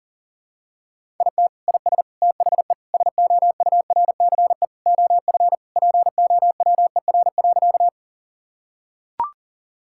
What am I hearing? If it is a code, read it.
Morse code, heard as a keyed tone, about 40 wpm, 700 Hz. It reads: IT IS THE SOURCE OF POWER.
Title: Sentences from Top 1000 Words - Review Part 5 - 40wpm